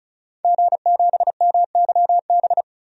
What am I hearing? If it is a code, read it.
Morse code, heard as a keyed tone, about 34 wpm, 700 Hz. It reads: G7MYB
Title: DXCC-100 Call Signs - 5 Characters - 35wpm